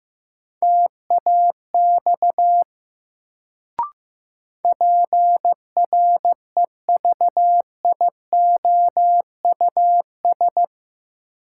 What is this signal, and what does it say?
Morse code, heard as a keyed tone, about 15 wpm, 700 Hz. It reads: TAX PREVIOUS
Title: Sentences from Top 1500 Words - New Word Drill Alt - 15wpm